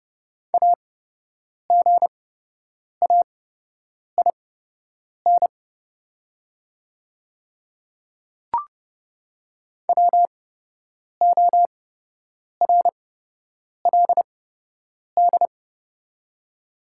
Morse code, heard as a keyed tone, about 30 wpm, spaced as for 8 wpm, 700 Hz. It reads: AGAIN WORLD
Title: 8x Character Spacing - Top 200 Words - 30wpm